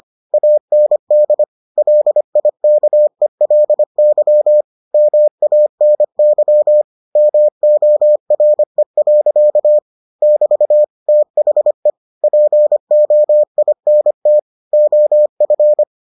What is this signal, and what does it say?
Morse code, heard as a keyed tone, about 25 wpm, 600 Hz. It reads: AND LIKELY MANY MORE. <BT> THE POINT OF